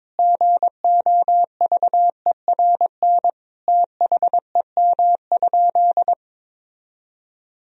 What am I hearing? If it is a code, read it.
Morse code, heard as a keyed tone, about 22 wpm, 700 Hz. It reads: GOVERN THEM?
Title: Sentences from Top 1000 Words - Review Part 2 - 22wpm